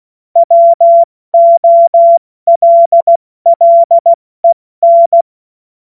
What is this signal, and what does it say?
Morse code, heard as a keyed tone, about 16 wpm, 680 Hz. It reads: WOLLEN
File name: Morsekurs / Witze / Witz 29. November 2022 70 F80